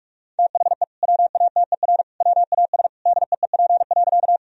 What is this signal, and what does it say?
Morse code, heard as a keyed tone, about 46 wpm, 700 Hz. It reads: THE WATER WAS DEEP.
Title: Sentences from Top 1000 Words - Review Part 3 - 45wpm